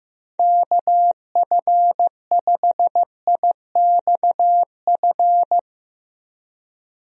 Morse code, heard as a keyed tone, about 15 wpm, 700 Hz. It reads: KF5IXF
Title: USA Call Signs - 6 Characters - 15wpm